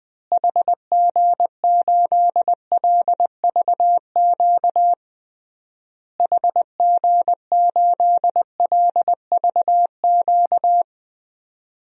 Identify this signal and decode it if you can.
Morse code, heard as a keyed tone, about 21 wpm, 700 Hz. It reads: HG8LVQ HG8LVQ
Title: DXCC-100 Call Signs - 6 Characters - Repeated 3x - 20wpm